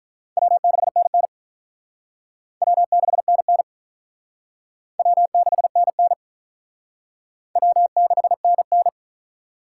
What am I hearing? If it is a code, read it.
Morse code, heard as a keyed tone, about 44 wpm, 700 Hz. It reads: W6NN W6NN W6NN W6NN
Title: Speed Racing - USA Call Signs 35wpm